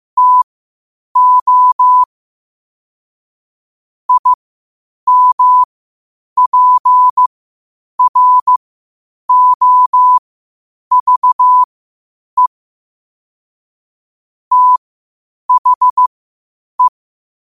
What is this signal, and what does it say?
Morse code, heard as a keyed tone, about 15 wpm, spaced as for 8 wpm, 1000 Hz. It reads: TO IMPROVE THE
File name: qotdfiles / 8wpm